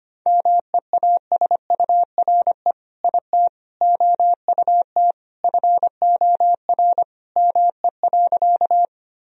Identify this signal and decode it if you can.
Morse code, heard as a keyed tone, about 25 wpm, 700 Hz. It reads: MEASURE IT OUT FOR ME.
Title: Sentences from Top 1000 Words - Review Part 2 - 25wpm